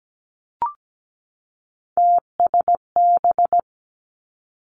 Morse code, heard as a keyed tone, about 17 wpm, 700 Hz. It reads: TSB
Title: Three Challenging Letter-Number Combinations 17wpm